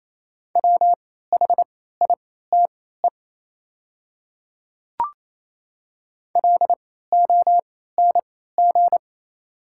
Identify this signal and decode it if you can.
Morse code, heard as a keyed tone, about 28 wpm, spaced as for 16 wpm, 700 Hz. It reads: WHITE LONG
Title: 3x Character Spacing - Top 300 Words - 28wpm